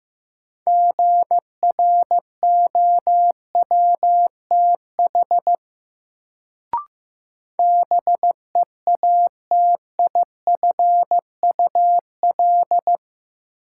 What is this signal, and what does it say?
Morse code, heard as a keyed tone, about 15 wpm, 700 Hz. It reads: GROWTH BEATIFUL
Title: Sentences from Top 1300 Words - New Word Drill Alt - 15wpm